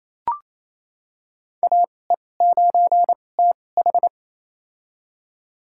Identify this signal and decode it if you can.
Morse code, heard as a keyed tone, about 28 wpm, spaced as for 20 wpm, 700 Hz. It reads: AE9TH
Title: USA Call Signs - 2x Character Spacing - 28wpm